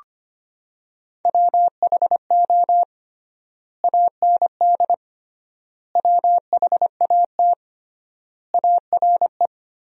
Morse code, heard as a keyed tone, about 25 wpm, 700 Hz. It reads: WHO AND WHAT ARE
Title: 3x Word-Spacing - Sentences from Top 100 Words - 25wpm